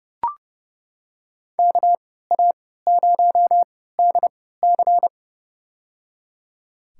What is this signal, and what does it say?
Morse code, heard as a keyed tone, about 30 wpm, spaced as for 17 wpm, 700 Hz. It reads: KA0DC
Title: USA Call Signs - 3x Character Spacing - 30wpm